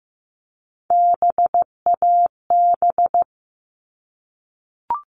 Morse code, heard as a keyed tone, about 15 wpm, 700 Hz. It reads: BAB
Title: All Cut Numbers - 3-Digit 15wpm